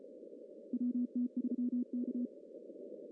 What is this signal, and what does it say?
Morse code, heard as a keyed tone, about 33 wpm, 240 Hz. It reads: WT3K
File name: CW / cqww-cw-k5tr